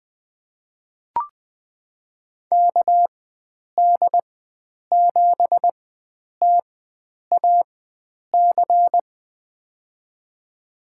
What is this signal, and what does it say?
Morse code, heard as a keyed tone, about 21 wpm, spaced as for 9 wpm, 700 Hz. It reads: KD7TAC
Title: USA Call Signs - 6 Characters - 4x Character Spacing - 20wpm